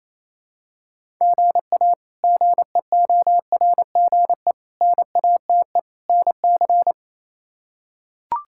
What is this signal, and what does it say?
Morse code, heard as a keyed tone, about 27 wpm, 700 Hz. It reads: GA GEORGE NATE NC
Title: K1USN SST QSO Practice - 28wpm